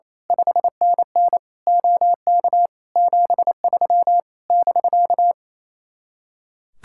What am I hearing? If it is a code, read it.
Morse code, heard as a keyed tone, about 26 wpm, 700 Hz. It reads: 5NN OK 73 <BK>
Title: Speed-Racing Common QSO Phrases - 28wpm